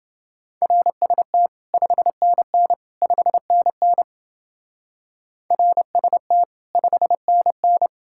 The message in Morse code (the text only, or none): RST 5NN 5NN RST 5NN